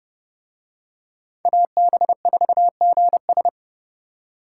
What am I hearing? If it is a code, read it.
Morse code, heard as a keyed tone, about 29 wpm, 700 Hz. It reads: AB4GS